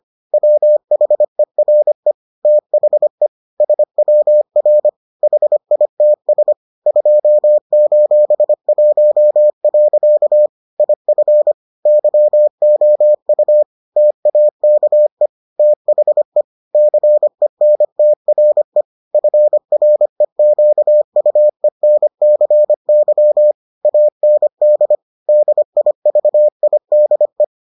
Morse code, heard as a keyed tone, about 25 wpm, 600 Hz. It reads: WHERE THE SWR HITS 2:1. IF YOU TAKE THE CENTRE FREQUENCY AND DIVIDE